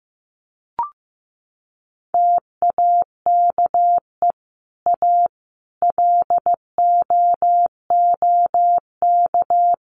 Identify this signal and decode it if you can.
Morse code, heard as a keyed tone, about 15 wpm, 700 Hz. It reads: TAKE A LOOK
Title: Sets of 5 Words Encore2 15wpm